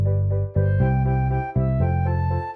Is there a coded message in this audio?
No.